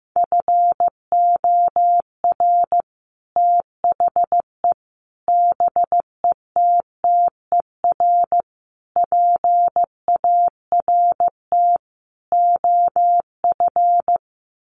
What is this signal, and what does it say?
Morse code, heard as a keyed tone, about 15 wpm, 700 Hz. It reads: FOR THE BETTER PART OF
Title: Sets of 5 Words Encore2 15wpm